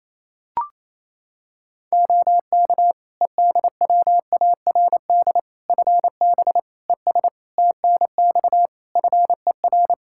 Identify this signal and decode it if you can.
Morse code, heard as a keyed tone, about 28 wpm, 700 Hz. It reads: OK EDWARD FB ES TNX FER
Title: Practice QSO - Exchange 2 - Info - 28wpm